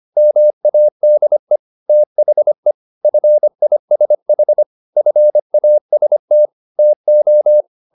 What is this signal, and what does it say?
Morse code, heard as a keyed tone, about 25 wpm, 600 Hz. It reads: MADE THE FISH FAST TO